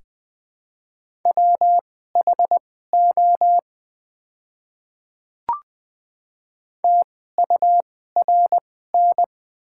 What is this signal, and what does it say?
Morse code, heard as a keyed tone, about 20 wpm, spaced as for 14 wpm, 700 Hz. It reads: WHO TURN
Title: 2x Character Spacing - Top 200 Words - 20wpm